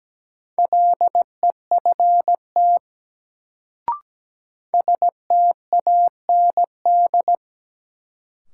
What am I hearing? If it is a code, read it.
Morse code, heard as a keyed tone, about 17 wpm, 700 Hz. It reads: LEFT STAND
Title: Sentences from Top 300 Words - New Word Drill - 17wpm